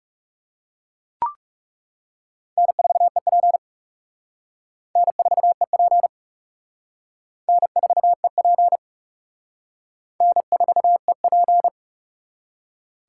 Morse code, heard as a keyed tone, about 35 wpm, 700 Hz. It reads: N4EP N4EP N4EP N4EP
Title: Speed Racing - USA 1x2 Call Signs 30wpm